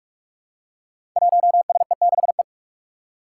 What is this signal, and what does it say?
Morse code, heard as a keyed tone, about 44 wpm, 700 Hz. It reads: 1SE6E
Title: Five Challenging Letter-Number Combinations 45wpm